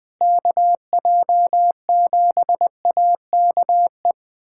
Morse code, heard as a keyed tone, about 20 wpm, 700 Hz. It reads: KJ7AKE
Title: USA Callsign DE Callsign - 6 Characters - 20wpm